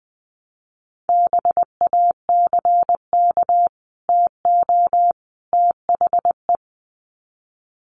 Morse code, heard as a keyed tone, about 20 wpm, 700 Hz. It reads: BACK TO THE
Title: Sets of 3 Words 20wpm